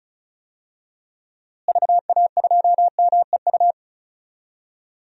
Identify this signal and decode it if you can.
Morse code, heard as a keyed tone, about 36 wpm, 700 Hz. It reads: VA2MEU
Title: DXCC-100 Call Signs - 6 Characters - Repeated 3x - 35wpm